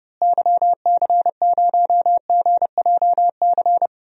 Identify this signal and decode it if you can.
Morse code, heard as a keyed tone, about 30 wpm, 700 Hz. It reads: YC0GJC
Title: DXCC-100 Call Signs - 6 Characters - Repeated 3x - 30wpm